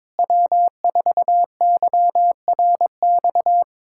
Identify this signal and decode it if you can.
Morse code, heard as a keyed tone, about 22 wpm, 700 Hz. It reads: W4YRX
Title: Speed-Racing Common QSO Phrases Encore - 22wpm